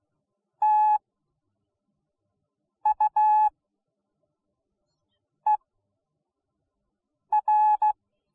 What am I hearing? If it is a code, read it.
Morse code, heard as a keyed tone, about 16 wpm, spaced as for 4 wpm, 850 Hz. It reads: TUER